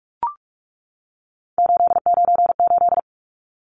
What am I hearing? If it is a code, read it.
Morse code, heard as a keyed tone, about 44 wpm, 700 Hz. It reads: 898
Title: Numbers - 3-Digit 45wpm